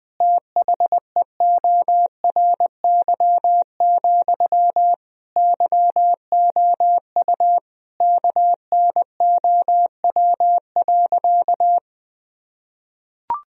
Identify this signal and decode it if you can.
Morse code, heard as a keyed tone, about 20 wpm, 700 Hz. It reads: THEORY, YOU KNOW.
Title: Sentences from Top 1000 Words - Review Part 3 - 20wpm